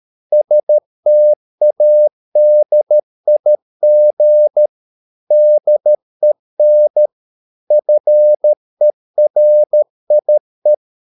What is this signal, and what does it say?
Morse code, heard as a keyed tone, about 13 wpm, 600 Hz. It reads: STADIG DEN FERIE
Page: Ferie 13 wpm